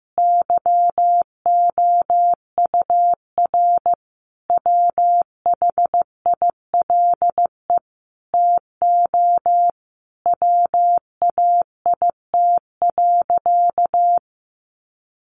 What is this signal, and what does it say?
Morse code, heard as a keyed tone, about 15 wpm, 700 Hz. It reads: YOUR WHILE TO WAIT.
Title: Sentences from Top 800 Words 15wpm